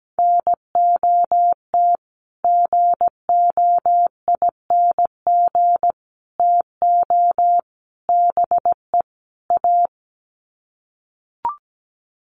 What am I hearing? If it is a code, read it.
Morse code, heard as a keyed tone, about 17 wpm, 700 Hz. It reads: NOT GOING TO BE A
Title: Sets of 5 Words Encore 17wpm